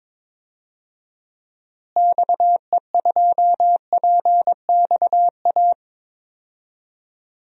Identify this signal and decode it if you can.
Morse code, heard as a keyed tone, about 21 wpm, 700 Hz. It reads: XE2PXA
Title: DXCC-100 Call Signs - 6 Characters - Repeated 3x - 22wpm